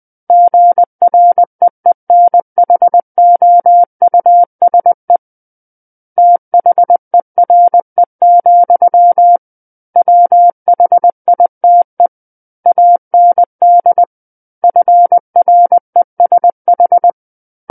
Morse code, heard as a keyed tone, about 20 wpm, 700 Hz. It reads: GREENHOUSE THERE, WHITE AND FRESH
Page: SKCC Learning Center - Morse Code Practice Files